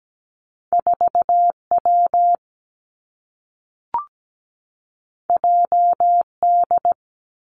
Morse code, heard as a keyed tone, about 17 wpm, 700 Hz. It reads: 4 W J D
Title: Two Challenging Letter-Number Combinations 17wpm